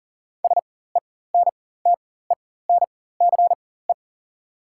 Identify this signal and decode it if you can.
Morse code, heard as a keyed tone, about 40 wpm, spaced as for 19 wpm, 690 Hz. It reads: SENTENCE
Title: 4x Character Spacing - Top 500 Words - 40wpm